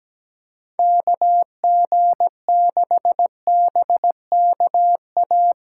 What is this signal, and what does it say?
Morse code, heard as a keyed tone, about 20 wpm, 700 Hz. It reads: KG6BKA
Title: Speed Racing - USA 2x3 Call Signs 17wpm